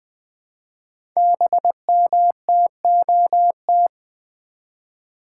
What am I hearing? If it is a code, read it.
Morse code, heard as a keyed tone, about 20 wpm, 700 Hz. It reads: BMTOT
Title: Code Groups - 5 Letters - 20wpm